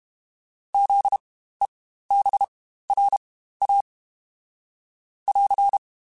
Morse code, heard as a keyed tone, about 32 wpm, spaced as for 15 wpm, 800 Hz. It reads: ZEBRA <AR>
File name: Spaced Alphas / Spaced Alphas mp3s / Zs sp @30